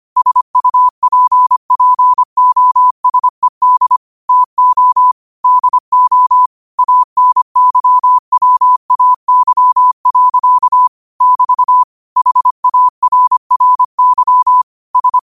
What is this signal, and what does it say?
Morse code, heard as a keyed tone, about 25 wpm, 1000 Hz. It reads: SUPPOSED TO DO ANYWAY. <BT> HARRY S
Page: New quotes every day in morse code at 25 Words per minute.